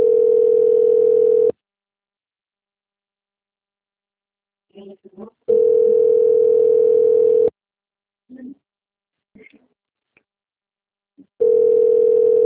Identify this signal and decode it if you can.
Ringback tone (North America), 440 + 480 Hz.